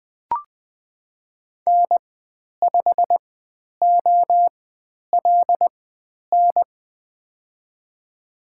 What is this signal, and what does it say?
Morse code, heard as a keyed tone, about 20 wpm, spaced as for 10 wpm, 700 Hz. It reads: N5OLN
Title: 095 Lesson 16g - Letter c - Call Signs 20wpm